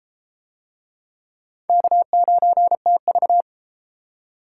Morse code, heard as a keyed tone, about 33 wpm, 700 Hz. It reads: K9TV